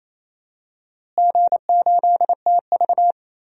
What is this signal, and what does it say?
Morse code, heard as a keyed tone, about 28 wpm, 700 Hz. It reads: G8TV
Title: DXCC-100 Call Signs - 4 Characters - Repeated 2x - 28wpm